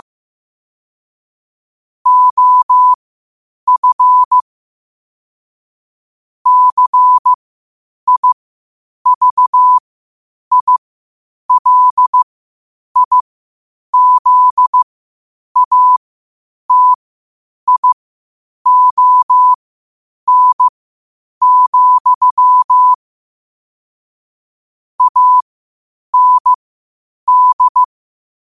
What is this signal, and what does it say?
Morse code, heard as a keyed tone, about 15 wpm, spaced as for 8 wpm, 1000 Hz. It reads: OF CIVILIZATION, AND
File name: qotdfiles / 8wpm